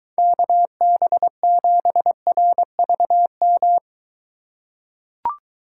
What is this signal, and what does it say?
Morse code, heard as a keyed tone, about 23 wpm, 700 Hz. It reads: KB7RVM